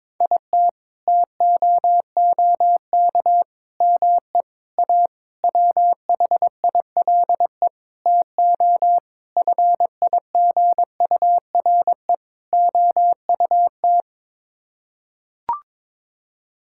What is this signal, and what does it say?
Morse code, heard as a keyed tone, about 22 wpm, 700 Hz. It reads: IT TOOK ME A WHILE TO FIGURE OUT
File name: practice / 22wpm / Sets of 8 Words - 22wpm